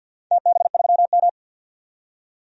Morse code, heard as a keyed tone, about 51 wpm, 700 Hz. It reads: TB3M
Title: DXCC-100 Call Signs - 4 Characters - 50wpm